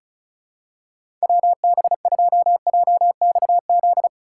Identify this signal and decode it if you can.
Morse code, heard as a keyed tone, about 36 wpm, 700 Hz. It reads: WB2JXZ
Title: USA Call Signs - 6 Characters - 3x Character Spacing - 35wpm